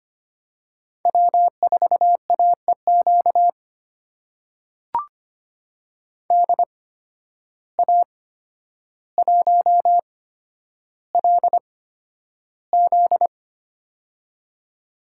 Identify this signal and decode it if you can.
Morse code, heard as a keyed tone, about 25 wpm, spaced as for 11 wpm, 700 Hz. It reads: W4AEQ DA1LZ
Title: USA Call Signs - 8x Character Spacing - 25wpm